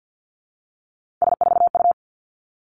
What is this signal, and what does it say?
Morse code, heard as a keyed tone, about 50 wpm, 700 Hz. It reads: S4U
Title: Three Challenging Letter-Number Combinations 50wpm